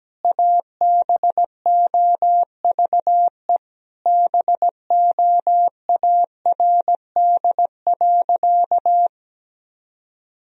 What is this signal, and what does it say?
Morse code, heard as a keyed tone, about 17 wpm, 700 Hz. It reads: ABOVE BOARD.
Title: Sentences from Top 1000 Words - Review Part 4 - 17wpm